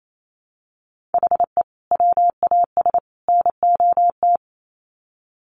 Morse code, heard as a keyed tone, about 28 wpm, 700 Hz. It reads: HE WAS NOT
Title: Sets of 3 Words 28wpm